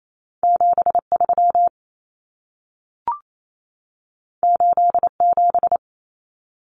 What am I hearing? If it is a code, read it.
Morse code, heard as a keyed tone, about 28 wpm, 700 Hz. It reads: 73 87
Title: Common Cut Numbers - 2-Digit 28wpm